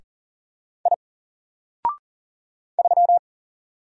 Morse code, heard as a keyed tone, about 40 wpm, 700 Hz. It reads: I 3
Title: Single Letter-Number Alt 40wpm